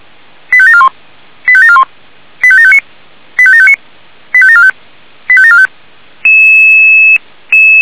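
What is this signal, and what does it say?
Five-tone selective call (ZVEI), heard as quick runs of five tones, tones of about 70 ms. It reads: 85741 85741 85759 85759 85745 85745
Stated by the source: ZVEI five-tone selective call 85727 received with an SDR.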